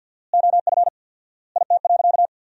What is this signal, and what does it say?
Morse code, heard as a keyed tone, about 48 wpm, 700 Hz. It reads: OF IT.
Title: 4x Word-Spacing - Sentences from Top 200 Words - 50wpm